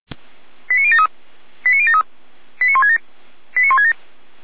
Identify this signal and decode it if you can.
Five-tone selective call (ZVEI), heard as quick runs of five tones, tones of about 70 ms. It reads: 89073 89073 89167 89167
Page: /455/SDR/ZVEI/89130/2025.11.26/